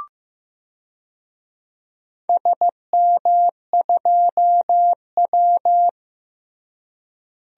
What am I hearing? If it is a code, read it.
Morse code, heard as a keyed tone, about 15 wpm, 700 Hz. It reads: SM2W
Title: DXCC-100 Call Signs - 4 Characters - Repeated 3x - 15wpm